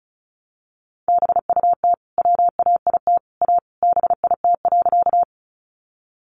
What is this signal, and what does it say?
Morse code, heard as a keyed tone, about 34 wpm, 700 Hz. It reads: BUT WAIT A BIT.
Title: Sentences from Top 900 Words 35wpm